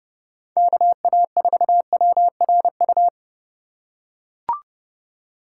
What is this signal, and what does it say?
Morse code, heard as a keyed tone, about 32 wpm, 700 Hz. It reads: KA4WRU
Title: USA Call Signs - 6 Characters - 3x Character Spacing - 30wpm